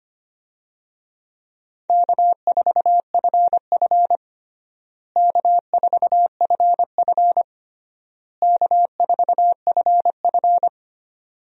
Morse code, heard as a keyed tone, about 25 wpm, 700 Hz. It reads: K4FF K4FF K4FF
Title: USA Call Signs - 4 Characters - Repeated 3x - 25wpm